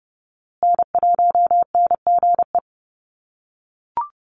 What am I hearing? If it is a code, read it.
Morse code, heard as a keyed tone, about 30 wpm, 700 Hz. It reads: N1NGE
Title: USA Call Signs Repeated 3x 30wpm